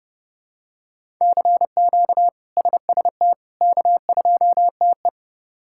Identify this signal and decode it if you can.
Morse code, heard as a keyed tone, about 30 wpm, 700 Hz. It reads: CQ SST K2TE
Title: K1USN SST QSO Practice - 30wpm